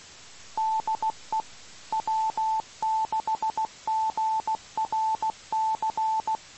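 Morse code, heard as a keyed tone, about 16 wpm, 880 Hz. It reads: DE W6GRC
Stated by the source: Morse received on 2 m at 147.105 MHz.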